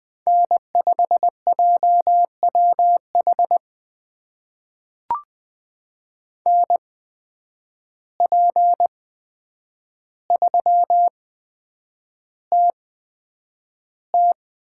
Morse code, heard as a keyed tone, about 20 wpm, spaced as for 9 wpm, 700 Hz. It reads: N5JWH NP3TT
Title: USA Call Signs - 5 Characters - 8x Character Spacing - 20wpm